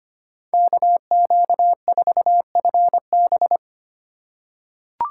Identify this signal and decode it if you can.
Morse code, heard as a keyed tone, about 25 wpm, 700 Hz. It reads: KQ4FB
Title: USA Call Signs - 5 Characters - 8x Character Spacing - 25wpm